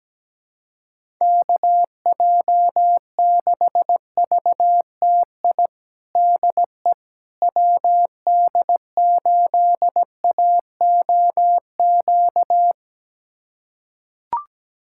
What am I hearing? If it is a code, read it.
Morse code, heard as a keyed tone, about 17 wpm, 700 Hz. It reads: KJ6VTI DE WD8AOQ
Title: USA Callsign DE Callsign - 6 Characters - 17wpm